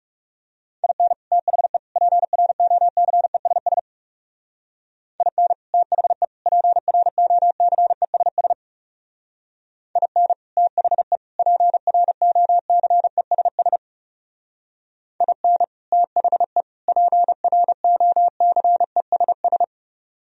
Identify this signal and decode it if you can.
Morse code, heard as a keyed tone, about 40 wpm, 700 Hz. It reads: IN THE PROCESS IN THE PROCESS IN THE PROCESS IN THE PROCESS